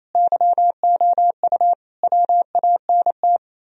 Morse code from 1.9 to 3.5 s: WANT

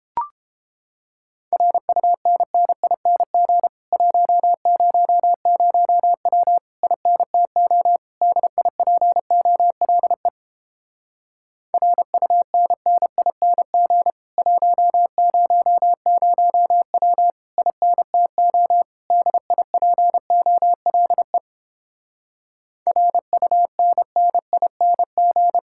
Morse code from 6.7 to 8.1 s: INTO

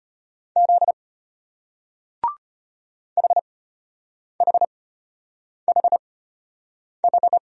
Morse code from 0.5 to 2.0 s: Z

